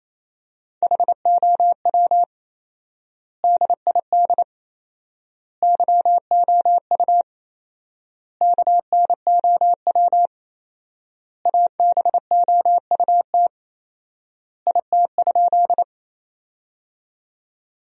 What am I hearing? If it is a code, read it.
Morse code, heard as a keyed tone, about 28 wpm, 700 Hz. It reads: HOW DID YOU KNOW ABOUT IT?
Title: 4x Word-Spacing - Sentences from Top 100 Words - 28wpm